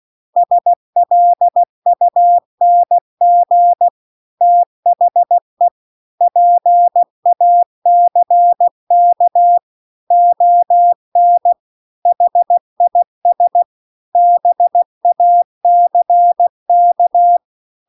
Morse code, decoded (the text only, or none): SLUNG THE PACK ON HIS BACK